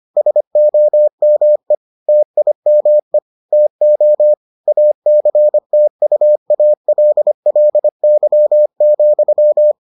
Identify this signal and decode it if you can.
Morse code, heard as a keyed tone, about 25 wpm, 600 Hz. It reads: SOME TIME TO ACTUALLY,